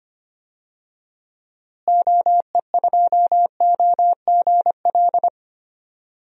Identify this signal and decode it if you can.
Morse code, heard as a keyed tone, about 25 wpm, 700 Hz. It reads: OE2OGL
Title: DXCC-100 Call Signs - 6 Characters - Repeated 2x - 25wpm